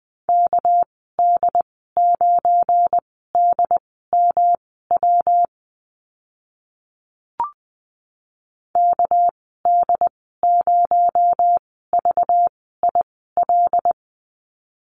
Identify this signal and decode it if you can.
Morse code, heard as a keyed tone, about 20 wpm, spaced as for 14 wpm, 700 Hz. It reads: KD9DMW KD0VIL
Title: 219 Lesson 35g - Number 0 - Call Signs 20wpm